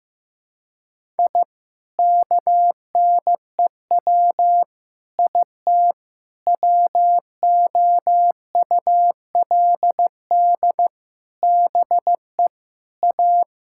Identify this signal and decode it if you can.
Morse code, heard as a keyed tone, about 15 wpm, 700 Hz. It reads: I KNEW IT WOULD BE A
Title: Sentences from Top 1000 Words - Review Part 6 - 15wpm